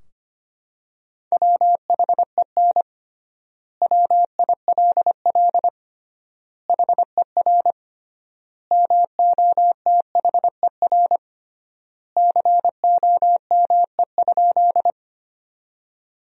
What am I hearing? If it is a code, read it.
Morse code, heard as a keyed tone, about 25 wpm, 700 Hz. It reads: WHEN WILL HER MOTHER COME?
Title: 3x Word-Spacing - Sentences from Top 200 Words - 25wpm